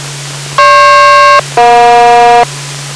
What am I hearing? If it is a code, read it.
Aircraft SELCAL, heard as two pairs of tones. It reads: GP-EJ